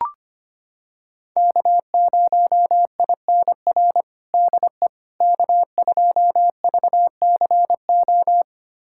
Morse code, heard as a keyed tone, about 25 wpm, 700 Hz. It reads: K0INR DE K2VCO
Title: USA Callsign DE Callsign - 5 Characters - 25wpm